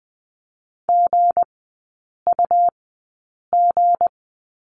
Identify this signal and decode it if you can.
Morse code, heard as a keyed tone, about 20 wpm, spaced as for 8 wpm, 700 Hz. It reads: GUG
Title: 159 Lesson 26e - Letter g - Three Characters 20wpm